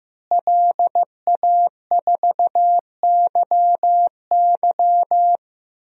Morse code, heard as a keyed tone, about 15 wpm, 700 Hz. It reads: LA4YY
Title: DXCC-100 Call Signs - 5 Characters - Repeated 2x - 15wpm